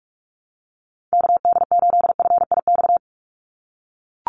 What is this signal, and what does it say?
Morse code, heard as a keyed tone, about 44 wpm, 700 Hz. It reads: KD8FIX
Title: USA Call Signs 45wpm